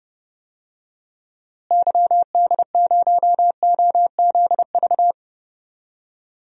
Morse code, heard as a keyed tone, about 30 wpm, 700 Hz. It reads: YD0OZV
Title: DXCC-100 Call Signs - 6 Characters - Repeated 2x - 30wpm